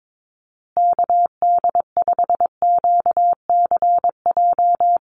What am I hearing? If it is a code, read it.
Morse code, heard as a keyed tone, about 22 wpm, 700 Hz. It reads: KD5QCJ
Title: USA Call Signs Repeated 3x 22wpm